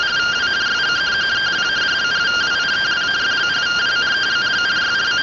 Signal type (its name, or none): two-tone teleprinter signal